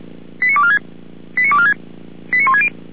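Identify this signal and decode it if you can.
Five-tone selective call (ZVEI), heard as quick runs of five tones, tones of about 70 ms. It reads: 89246 89246 89160
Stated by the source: ZVEI five-tone selective call received by SDR.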